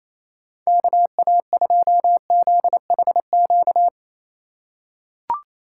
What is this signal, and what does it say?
Morse code, heard as a keyed tone, about 27 wpm, 700 Hz. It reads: KA2ZHQ
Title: USA Call Signs - 6 Characters - 3x Character Spacing - 28wpm